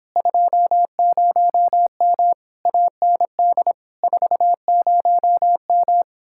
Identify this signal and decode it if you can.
Morse code, heard as a keyed tone, about 29 wpm, 700 Hz. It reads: 20M AND 40M